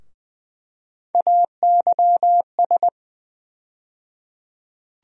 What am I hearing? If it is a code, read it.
Morse code, heard as a keyed tone, about 20 wpm, 700 Hz. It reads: AYS
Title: USA Call Sign Postfixes - 3 Letters 20wpm